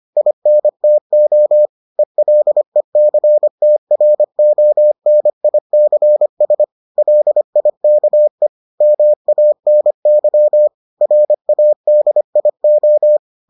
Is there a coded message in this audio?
Yes: Morse code from 0.2 to 13.2 s, heard as a keyed tone, about 25 wpm, 600 Hz. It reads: INTO ELECTRONICS LIKE MANY RADIO